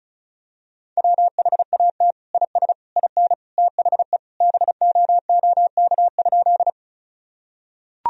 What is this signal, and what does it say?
Morse code, heard as a keyed tone, about 36 wpm, 700 Hz. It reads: WHAT IS IN THE BOOK?
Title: Sentences from Top 1000 Words - Review Part 1 - 35wpm